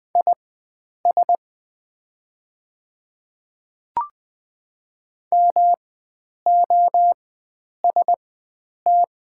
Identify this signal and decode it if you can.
Morse code, heard as a keyed tone, about 20 wpm, spaced as for 9 wpm, 700 Hz. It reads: IS MOST
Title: 4x Character Spacing - Top 100 Words - 20wpm